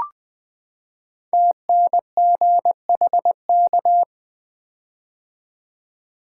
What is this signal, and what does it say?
Morse code, heard as a keyed tone, about 20 wpm, 700 Hz. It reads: TNGHK
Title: Code Groups - 5 Letter:Numbers - 20wpm